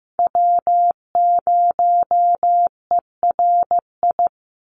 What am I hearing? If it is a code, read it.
Morse code, heard as a keyed tone, about 15 wpm, 700 Hz. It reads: W0ERI
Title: NA SOTA Chaser Callsigns 15wpm